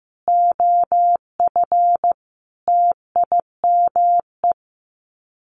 Morse code, heard as a keyed tone, about 15 wpm, 700 Hz. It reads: OF TIME
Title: Sets of 5 Words Encore 15wpm